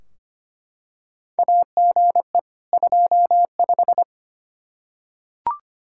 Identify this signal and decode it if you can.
Morse code, heard as a keyed tone, about 25 wpm, 700 Hz. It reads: AGE 25
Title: Practice QSO - Exchange 3 - More Info - 25wpm